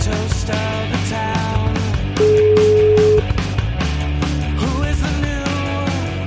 Ringback tone (Europe, 425 Hz), with noise.